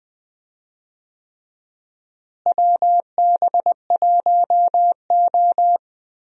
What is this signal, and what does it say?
Morse code, heard as a keyed tone, about 20 wpm, 700 Hz. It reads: WB1O
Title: USA Call Signs - 4 Characters - Repeated 3x - 20wpm